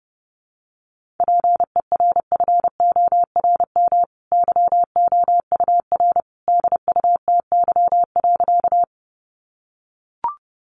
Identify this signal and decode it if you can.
Morse code, heard as a keyed tone, about 30 wpm, 700 Hz. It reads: PERFORM YOUR DUTY.
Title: Sentences from Top 1500 Words 30wpm